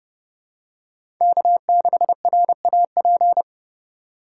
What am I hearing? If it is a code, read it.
Morse code, heard as a keyed tone, about 30 wpm, 700 Hz. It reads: K6RAP
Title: USA Call Signs - 5 Characters - 3x Character Spacing - 30wpm